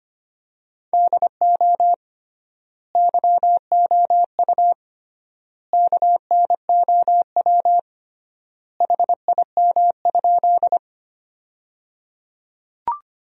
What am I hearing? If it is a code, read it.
Morse code, heard as a keyed tone, about 25 wpm, 700 Hz. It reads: DO YOU KNOW HIM?
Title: 3x Word-Spacing - Sentences from Top 100 Words - 25wpm